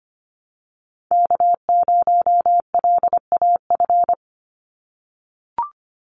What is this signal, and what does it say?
Morse code, heard as a keyed tone, about 25 wpm, 700 Hz. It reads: K0LAF
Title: NA SOTA Chaser Callsigns 25wpm